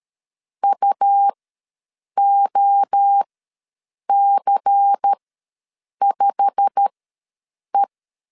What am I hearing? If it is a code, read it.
Morse code, heard as a keyed tone, about 13 wpm, spaced as for 7 wpm, 800 Hz. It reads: UOC5E
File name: LEZIONI / 8 YJ45 / 8 YJ45 ws 12wpm 3146